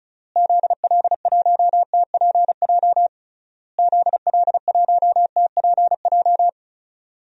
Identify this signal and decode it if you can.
Morse code, heard as a keyed tone, about 36 wpm, 700 Hz. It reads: ZL1TPJ ZL1TPJ
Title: DXCC-100 Call Signs - 6 Characters - Repeated 3x - 35wpm